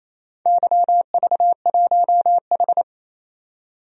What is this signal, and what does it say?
Morse code, heard as a keyed tone, about 28 wpm, 700 Hz. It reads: YV1H
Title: DXCC-100 Call Signs - 4 Characters - 28wpm